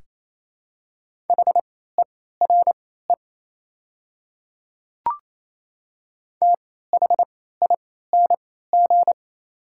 Morse code, heard as a keyed tone, about 28 wpm, spaced as for 16 wpm, 700 Hz. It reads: HERE THING
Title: 3x Character Spacing - Top 300 Words - 28wpm